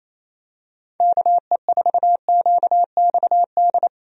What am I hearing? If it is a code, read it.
Morse code, heard as a keyed tone, about 28 wpm, 700 Hz. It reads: KE4QXD
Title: USA Call Signs - 6 Characters - Repeated 3x - 28wpm